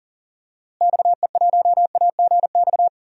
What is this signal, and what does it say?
Morse code, heard as a keyed tone, about 40 wpm, 700 Hz. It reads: XE1AGX
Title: DXCC-100 Call Signs - 6 Characters - Repeated 2x - 40wpm